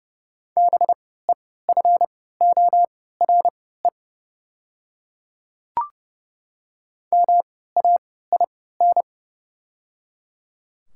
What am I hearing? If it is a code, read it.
Morse code, heard as a keyed tone, about 30 wpm, spaced as for 17 wpm, 700 Hz. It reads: BEFORE MAIN
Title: 3x Character Spacing - Top 300 Words - 30wpm